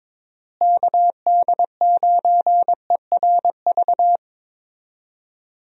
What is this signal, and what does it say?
Morse code, heard as a keyed tone, about 21 wpm, 700 Hz. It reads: KD9ERV
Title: USA Call Signs - 6 Characters - Repeated 3x - 22wpm